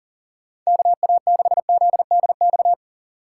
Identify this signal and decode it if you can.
Morse code, heard as a keyed tone, about 40 wpm, 700 Hz. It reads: KA6ZDX
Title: USA Call Signs - 2x Character Spacing - 40wpm